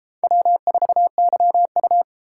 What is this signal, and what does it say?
Morse code, heard as a keyed tone, about 32 wpm, 700 Hz. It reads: W4YU